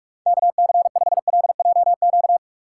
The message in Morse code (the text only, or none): KC5LJQ